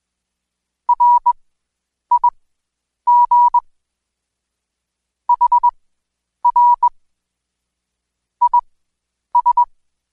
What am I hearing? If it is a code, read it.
Morse code, heard as a keyed tone, about 21 wpm, spaced as for 9 wpm, 970 Hz. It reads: RIG HR IS
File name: Code Quick / AudioDownld2 / Seg115-Day20-DayDrill 20